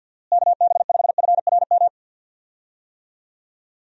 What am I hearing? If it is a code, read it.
Morse code, heard as a keyed tone, about 49 wpm, 700 Hz. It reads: KB5FRM
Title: USA Call Signs - 6 Characters - Repeated 2x - 50wpm